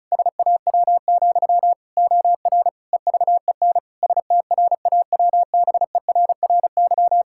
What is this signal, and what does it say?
Morse code, heard as a keyed tone, about 36 wpm, 700 Hz. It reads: SAW, OR EVEN STRAWBERRY
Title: Pangram Challenge v2 - 35wpm